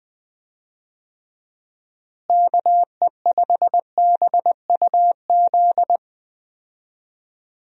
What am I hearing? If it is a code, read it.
Morse code, heard as a keyed tone, about 20 wpm, 700 Hz. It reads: KE5BUZ
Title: USA Call Signs - 6 Characters - Repeated 2x - 20wpm